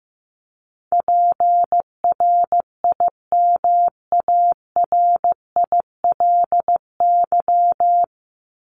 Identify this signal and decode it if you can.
Morse code, heard as a keyed tone, about 15 wpm, 700 Hz. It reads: PRIMARILY
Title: Top 1001-1250 Words 15wpm